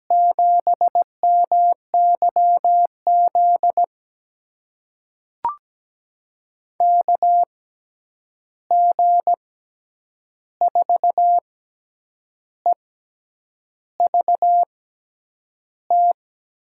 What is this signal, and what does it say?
Morse code, heard as a keyed tone, about 17 wpm, spaced as for 8 wpm, 700 Hz. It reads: 7MYZ KG4EVT
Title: USA Call Signs - 6 Characters - 6x Character Spacing - 17wpm